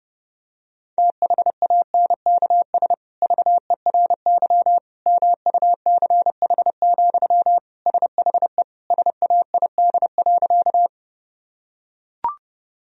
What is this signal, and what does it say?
Morse code, heard as a keyed tone, about 30 wpm, 700 Hz. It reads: THANKS VERY MUCH, SHE SAID.
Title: Sentences from Top 1000 Words - Review Part 3 - 30wpm